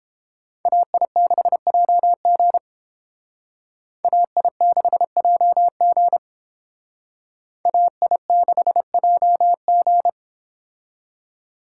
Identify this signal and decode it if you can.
Morse code, heard as a keyed tone, about 29 wpm, 700 Hz. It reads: AI6JG AI6JG AI6JG